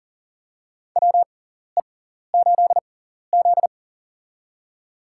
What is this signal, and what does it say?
Morse code, heard as a keyed tone, about 40 wpm, spaced as for 14 wpm, 700 Hz. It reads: WE8Z
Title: USA Call Signs - 6x Character Spacing - 40wpm